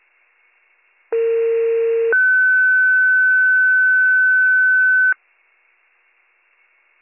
Two-tone page, heard as a one-second tone, then a three-second tone, about 460 then 1500 Hz. It is heard from 1.1 to 5.1 s.